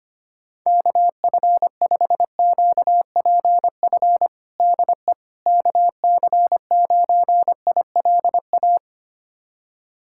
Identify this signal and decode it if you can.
Morse code, heard as a keyed tone, about 25 wpm, 700 Hz. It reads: KF5QPF DE KC9ILA
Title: USA Callsign DE Callsign 25wpm